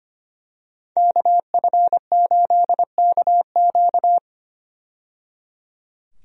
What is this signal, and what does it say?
Morse code, heard as a keyed tone, about 25 wpm, 700 Hz. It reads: KF8KQ
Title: USA Call Signs - 5 Characters - 25wpm